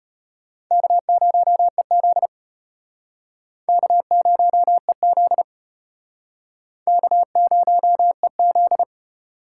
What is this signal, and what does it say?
Morse code, heard as a keyed tone, about 34 wpm, 700 Hz. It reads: K0EZ K0EZ K0EZ